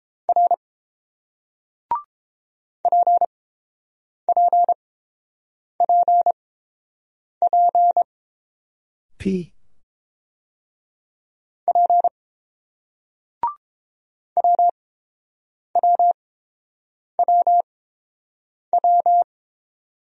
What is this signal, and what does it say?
Morse code, heard as a keyed tone, about 30 wpm, spaced as for 6 wpm, 700 Hz. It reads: R PPPPP WWWW